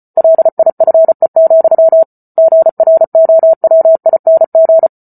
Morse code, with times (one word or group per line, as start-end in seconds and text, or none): 0.2-2.0 LIFE,
2.4-4.9 GROWING